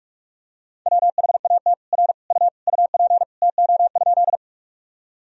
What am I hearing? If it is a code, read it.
Morse code, heard as a keyed tone, about 46 wpm, 700 Hz. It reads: WHAT R U UP TO?
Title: Sentences from Top 100 Words and Abbreviations 45wpm